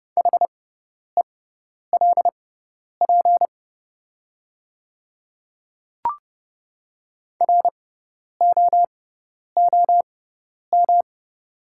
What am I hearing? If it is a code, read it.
Morse code, heard as a keyed tone, about 30 wpm, spaced as for 10 wpm, 700 Hz. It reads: HELP ROOM
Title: 6x Character Spacing - Top 500 Words - 30wpm